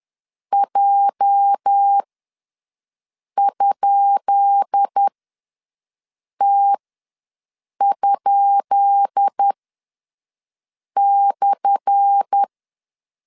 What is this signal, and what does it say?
Morse code, heard as a keyed tone, about 10 wpm, spaced as for 5 wpm, 800 Hz. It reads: J?T?/
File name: LEZIONI / 10 SIGNS / 10B INT ws 10wpm 4134